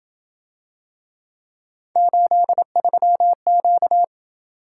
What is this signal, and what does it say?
Morse code, heard as a keyed tone, about 27 wpm, 700 Hz. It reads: 83Q